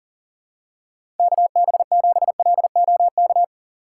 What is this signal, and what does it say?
Morse code, heard as a keyed tone, about 40 wpm, 700 Hz. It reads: KB7LOK
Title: USA Call Signs - 6 Characters - 6x Character Spacing - 40wpm